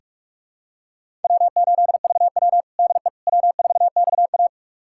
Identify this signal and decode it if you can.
Morse code, heard as a keyed tone, about 42 wpm, 700 Hz. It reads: W8VW DE W4XA
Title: USA Callsign DE Callsign - 4 Characters - 45wpm